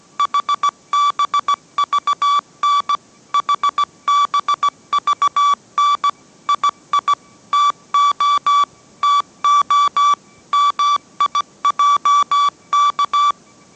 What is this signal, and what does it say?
Morse code, heard as a keyed tone, about 17 wpm, 1150 Hz. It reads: HBVN HBVN II TO TO MIJK